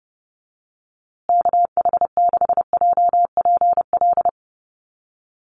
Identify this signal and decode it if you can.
Morse code, heard as a keyed tone, about 30 wpm, 700 Hz. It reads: KH6JPL
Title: USA Call Signs Repeated 3x 30wpm